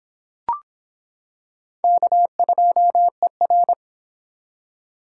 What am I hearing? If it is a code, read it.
Morse code, heard as a keyed tone, about 25 wpm, 700 Hz. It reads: K2ER